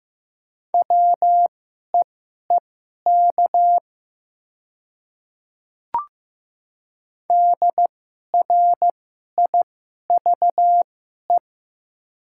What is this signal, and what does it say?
Morse code, heard as a keyed tone, about 15 wpm, spaced as for 11 wpm, 700 Hz. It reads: WEEK DRIVE
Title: 2x Character Spacing - Top 500 Words - 15wpm